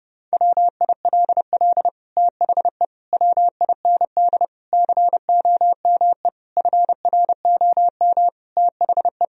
Morse code, heard as a keyed tone, about 32 wpm, 700 Hz. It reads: WILL THE WIND COME FROM THE
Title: Sentences from Top 1000 Words - Review Part 2 - 30wpm